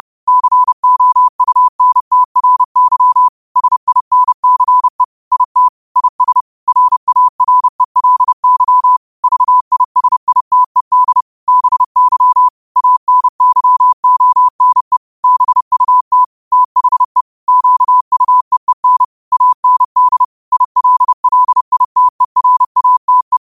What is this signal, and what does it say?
Morse code, heard as a keyed tone, about 30 wpm, 1000 Hz. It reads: COUNTRY SINCE IT IS RARELY VISITED BY ANYONE BUT THE QUEEN AND ILLITERATE